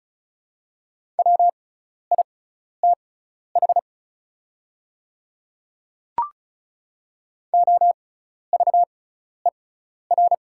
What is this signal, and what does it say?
Morse code, heard as a keyed tone, about 36 wpm, spaced as for 12 wpm, 690 Hz. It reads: WITH OVER
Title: 6x Character Spacing - Top 100 Words - 35wpm